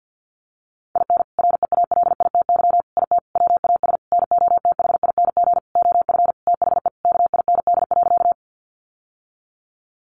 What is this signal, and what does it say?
Morse code, heard as a keyed tone, about 48 wpm, 700 Hz. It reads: IN REALITY IT WAS NOTHING OF THE KIND.
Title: Sentences from Top 1200 Words 50wpm